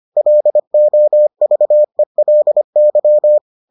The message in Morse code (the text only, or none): LOVELY